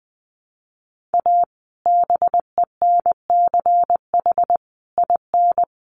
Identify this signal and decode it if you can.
Morse code, heard as a keyed tone, about 20 wpm, 700 Hz. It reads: A BENCH IN
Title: Sentences from Top 1400 Words 20wpm